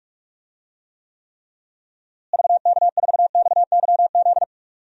Morse code, heard as a keyed tone, about 44 wpm, 700 Hz. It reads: VK4XYZ
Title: DXCC-100 Call Signs - 6 Characters - Repeated 3x - 45wpm